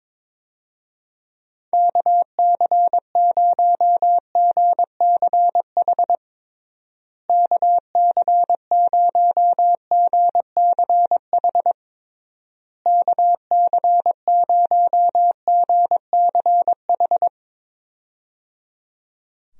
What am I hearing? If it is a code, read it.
Morse code, heard as a keyed tone, about 22 wpm, 700 Hz. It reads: KC0GCH KC0GCH KC0GCH
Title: USA Call Signs - 6 Characters - Repeated 3x - 22wpm